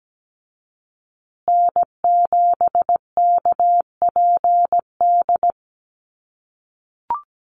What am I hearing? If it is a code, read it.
Morse code, heard as a keyed tone, about 17 wpm, 700 Hz. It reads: N7KPD
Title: USA Call Signs Repeated 3x 17wpm